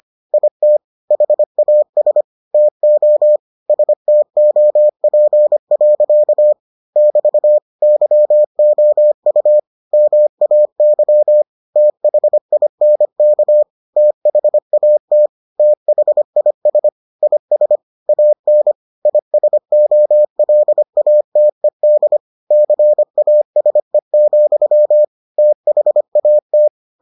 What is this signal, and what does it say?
Morse code, heard as a keyed tone, about 25 wpm, 600 Hz. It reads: IT HAS TO STOP. <BT> YOU MAY THINK THAT THIS IS AN ISOLATED CASE, THAT